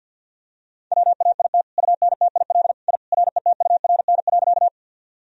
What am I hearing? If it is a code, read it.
Morse code, heard as a keyed tone, about 52 wpm, 700 Hz. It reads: WAIT UNTIL I RETURN.
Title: Sentences from Top 1000 Words - Review Part 1 - 50wpm